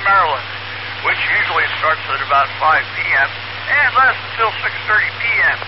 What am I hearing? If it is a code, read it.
Emergency Alert System (SAME) tones, heard as harsh buzzing bursts, with speech. It is heard from 1.0 to 5.2 s.